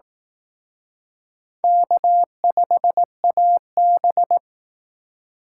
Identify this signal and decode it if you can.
Morse code, heard as a keyed tone, about 18 wpm, 700 Hz. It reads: K5AB
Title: Speed Racing - USA 1x2 Call Signs 15wpm